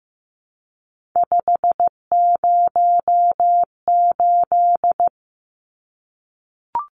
Morse code, heard as a keyed tone, about 15 wpm, 700 Hz. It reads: 508
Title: Numbers - 3-Digit 15wpm